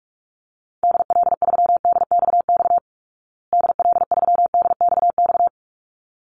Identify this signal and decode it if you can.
Morse code, heard as a keyed tone, about 44 wpm, 700 Hz. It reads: DL3DXX DL3DXX
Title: Call Signs Repeated 3x - 45wpm